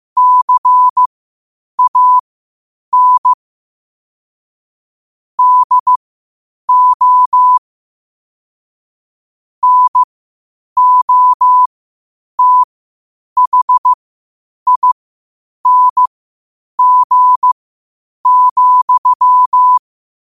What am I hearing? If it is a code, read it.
Morse code, heard as a keyed tone, about 15 wpm, spaced as for 8 wpm, 1000 Hz. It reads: CAN DO NOTHING,
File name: qotdfiles / 8wpm